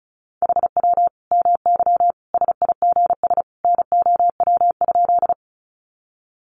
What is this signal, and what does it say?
Morse code, heard as a keyed tone, about 34 wpm, 700 Hz. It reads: HW MY SIGS NOW?
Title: Common QSO Phrases 35wpm